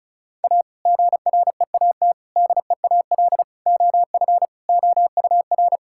Morse code, heard as a keyed tone, about 36 wpm, 700 Hz. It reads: A GREAT DEAL OF OUR
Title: Sentences from Top 1000 Words - Review Part 6 - 35wpm